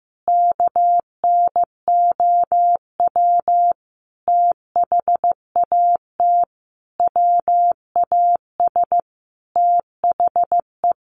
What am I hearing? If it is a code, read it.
Morse code, heard as a keyed tone, about 15 wpm, 700 Hz. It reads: KNOW THAT WAS THE
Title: Sentences from Top 1500 Words 15wpm